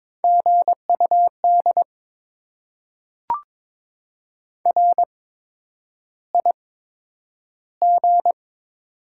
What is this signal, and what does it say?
Morse code, heard as a keyed tone, about 22 wpm, spaced as for 9 wpm, 700 Hz. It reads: GUD RIG
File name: Top 20 QSO Elements - 8x Character Spacing - 22wpm